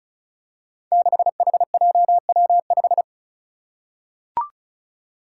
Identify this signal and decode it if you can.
Morse code, heard as a keyed tone, about 36 wpm, 700 Hz. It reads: 6HJW5
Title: Five Challenging Letter-Number Combinations 35wpm